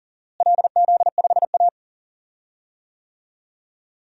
Morse code, heard as a keyed tone, about 40 wpm, 700 Hz. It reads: LZ5A
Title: DXCC-100 Call Signs - 4 Characters - Repeated 2x - 40wpm